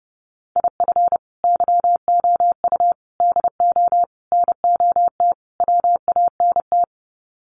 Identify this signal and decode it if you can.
Morse code, heard as a keyed tone, about 30 wpm, 700 Hz. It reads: IF YOU DO NOT WANT
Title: Sets of 5 Words 30wpm